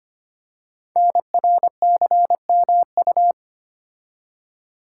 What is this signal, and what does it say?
Morse code, heard as a keyed tone, about 25 wpm, 700 Hz. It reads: NRCMU
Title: Code Groups - 5 Letters - 25wpm